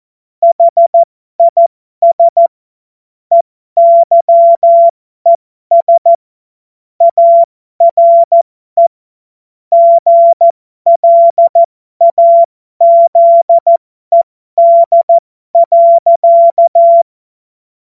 Morse code, heard as a keyed tone, about 14 wpm, spaced as for 12 wpm, 680 Hz. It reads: HIS EYES ARE GLAZED.
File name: Morsekurs / Witze